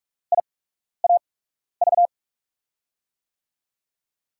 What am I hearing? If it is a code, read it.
Morse code, heard as a keyed tone, about 46 wpm, spaced as for 12 wpm, 700 Hz. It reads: IAV
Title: USA Call Signs - 6 Characters - 8x Character Spacing - 45wpm